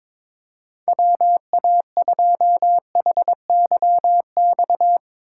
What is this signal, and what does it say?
Morse code, heard as a keyed tone, about 22 wpm, 700 Hz. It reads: WA2HYX
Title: USA Call Signs - 6 Characters - 3x Character Spacing - 22wpm